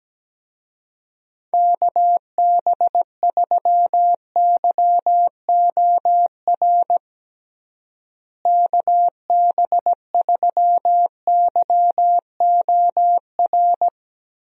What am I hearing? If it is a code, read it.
Morse code, heard as a keyed tone, about 17 wpm, 700 Hz. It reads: KB3YOR KB3YOR
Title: USA Call Signs - 6 Characters - Repeated 2x - 17wpm